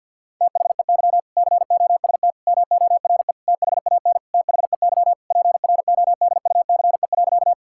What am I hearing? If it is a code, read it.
Morse code, heard as a keyed tone, about 50 wpm, 700 Hz. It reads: THEY COST MORE THAN THEY PRODUCE.
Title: Sentences from Top 1000 Words - Review Part 5 - 50wpm